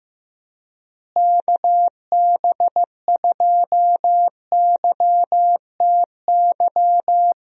Morse code, heard as a keyed tone, about 15 wpm, 700 Hz. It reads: KB2YTY
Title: USA Call Signs - 6 Characters - Repeated 3x - 15wpm